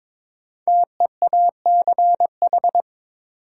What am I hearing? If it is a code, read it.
Morse code, heard as a keyed tone, about 23 wpm, 700 Hz. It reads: TEACH
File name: 5 Letter Words - First 3 Letters Given - 22wpm